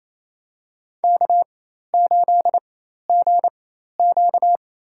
Morse code, heard as a keyed tone, about 28 wpm, spaced as for 13 wpm, 700 Hz. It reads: K8GQ
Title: USA Call Signs - 4 Characters - 4x Character Spacing - 28wpm